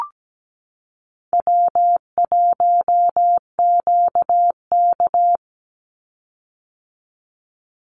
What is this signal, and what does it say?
Morse code, heard as a keyed tone, about 17 wpm, 700 Hz. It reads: W1QK
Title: Easy Call Signs - 17wpm